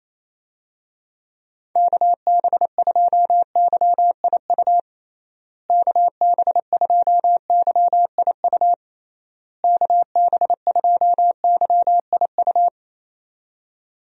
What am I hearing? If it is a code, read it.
Morse code, heard as a keyed tone, about 27 wpm, 700 Hz. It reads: KB2YIU KB2YIU KB2YIU
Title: USA Call Signs - 6 Characters - Repeated 3x - 28wpm